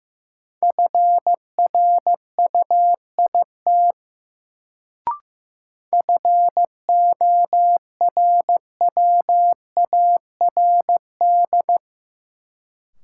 Morse code, heard as a keyed tone, about 15 wpm, 700 Hz. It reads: FRUIT FORWARD
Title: Sentences from Top 1300 Words - New Word Drill - 15wpm